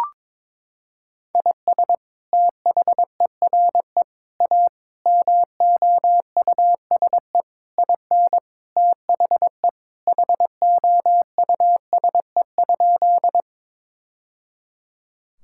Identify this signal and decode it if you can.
Morse code, heard as a keyed tone, about 22 wpm, 700 Hz. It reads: IS THERE A MOUSE IN THE HOUSE?
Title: Sentences from Top 1000 Words - Review Part 5 - 22wpm